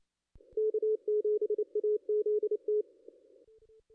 Morse code, heard as a keyed tone, about 29 wpm, 420 Hz. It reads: K7AZT
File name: contests / ss-cw-k5tr-op-n5rz -2300